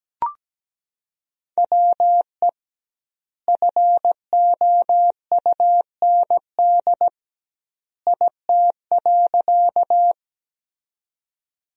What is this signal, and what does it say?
Morse code, heard as a keyed tone, about 17 wpm, 700 Hz. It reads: WE FOUND IT.
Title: 2x Word-Spacing - Sentences from Top 300 Words - 17wpm